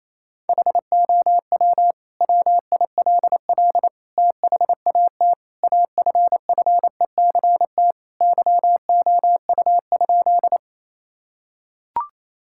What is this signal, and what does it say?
Morse code, heard as a keyed tone, about 28 wpm, 700 Hz. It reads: HOW WILL THAT AFFECT YOU?
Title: Sentences from Top 1000 Words - Review Part 6 - 28wpm